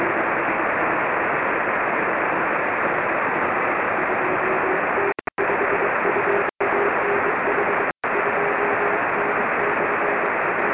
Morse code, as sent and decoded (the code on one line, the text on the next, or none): none